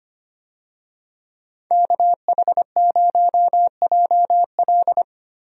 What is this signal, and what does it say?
Morse code, heard as a keyed tone, about 25 wpm, 700 Hz. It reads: KH0JL
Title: USA Call Signs - 5 Characters - 25wpm